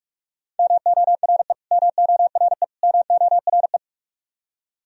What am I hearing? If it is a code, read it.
Morse code, heard as a keyed tone, about 46 wpm, 700 Hz. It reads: MORE MORE MORE
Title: Top 100 Words Repeated 45wpm